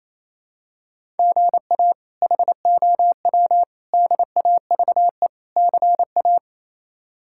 Morse code, heard as a keyed tone, about 27 wpm, 700 Hz. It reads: GA HOW DAVE CA
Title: K1USN SST QSO Practice - 28wpm